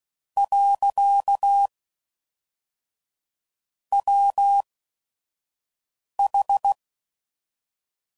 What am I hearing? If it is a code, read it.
Morse code, heard as a keyed tone, about 16 wpm, spaced as for 5 wpm, 800 Hz. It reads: . WH
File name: CW Code Groups @5WPM / Technical Subjects / AND Gate